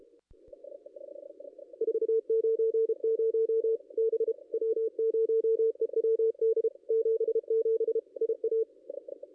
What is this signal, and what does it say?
Morse code, heard as a keyed tone, about 30 wpm, 440 Hz. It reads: 490 B W0EWD 77 IA
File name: contests / ss-cw-k5tr-op-n5rz -0400